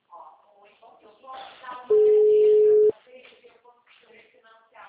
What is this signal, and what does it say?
Ringback tone (Europe, 425 Hz), with speech.